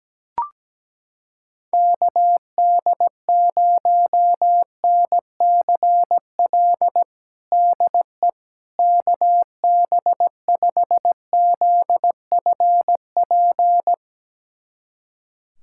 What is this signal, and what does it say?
Morse code, heard as a keyed tone, about 17 wpm, 700 Hz. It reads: KD0NCL DE KB5ZFP
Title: USA Callsign DE Callsign - 6 Characters - 17wpm